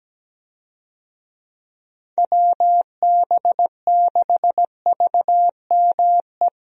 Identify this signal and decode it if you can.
Morse code, heard as a keyed tone, about 17 wpm, 700 Hz. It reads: WB6VME
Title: USA Call Signs - 6 Characters - Repeated 2x - 17wpm